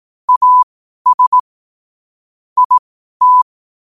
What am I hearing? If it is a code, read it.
Morse code, heard as a keyed tone, about 18 wpm, spaced as for 12 wpm, 1000 Hz. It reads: AS IT
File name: qotdfiles / 12wpm